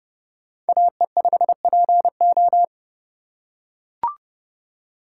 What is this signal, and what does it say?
Morse code, heard as a keyed tone, about 28 wpm, 700 Hz. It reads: AE5PO